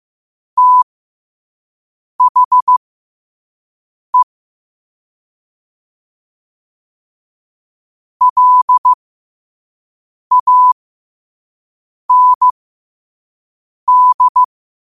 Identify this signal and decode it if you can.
Morse code, heard as a keyed tone, about 15 wpm, spaced as for 5 wpm, 1000 Hz. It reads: THE LAND